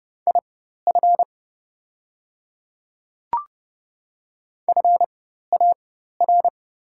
Morse code, heard as a keyed tone, about 30 wpm, spaced as for 14 wpm, 700 Hz. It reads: IF FAR
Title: 4x Character Spacing - Top 300 Words - 30wpm